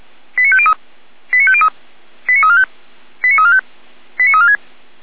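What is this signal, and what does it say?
Five-tone selective call (ZVEI), heard as quick runs of five tones, tones of about 70 ms. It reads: 89503 89503 89345 89345 89346
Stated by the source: ZVEI five-tone selective call 89305 received by SDR, Feuerwehr Orlamünde audio server.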